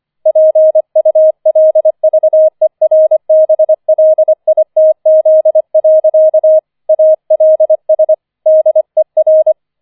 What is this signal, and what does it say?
Morse code, heard as a keyed tone, about 25 wpm, 630 Hz. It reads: PULVERBLITZ. ALS DER
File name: 004 Vierter Streich 25wpm